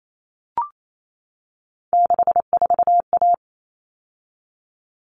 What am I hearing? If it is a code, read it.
Morse code, heard as a keyed tone, about 28 wpm, 700 Hz. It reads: 64A
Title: All Cut Numbers - 3-Digit 28wpm